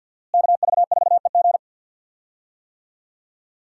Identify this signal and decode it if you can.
Morse code, heard as a keyed tone, about 47 wpm, 700 Hz. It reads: KV4EG
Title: USA Call Signs - 5 Characters - Repeated 3x - 50wpm